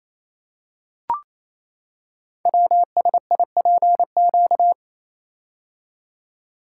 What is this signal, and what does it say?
Morse code, heard as a keyed tone, about 29 wpm, 700 Hz. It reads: WSIPQ
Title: Code Groups - 5 Letters - 28wpm